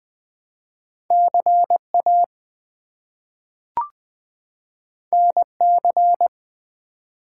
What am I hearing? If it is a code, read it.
Morse code, heard as a keyed tone, about 20 wpm, 700 Hz. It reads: CA NC
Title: US States and Canadian Provinces 20wpm